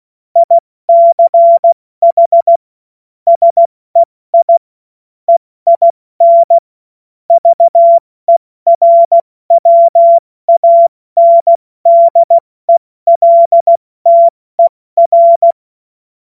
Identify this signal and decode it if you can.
Morse code, heard as a keyed tone, about 16 wpm, 680 Hz. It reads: ICH SEI EIN VERWANDELTER
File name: Morsekurs / Witze / Witz 08. November 2022 70 F80